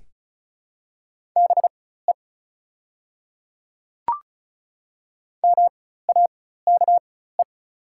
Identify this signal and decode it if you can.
Morse code, heard as a keyed tone, about 36 wpm, spaced as for 16 wpm, 690 Hz. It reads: BE MAKE
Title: 4x Character Spacing - Top 100 Words - 35wpm